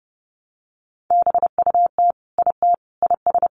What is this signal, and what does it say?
Morse code, heard as a keyed tone, about 28 wpm, 700 Hz. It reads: BUT IT IS
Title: Sentences from Top 1100 Words 30wpm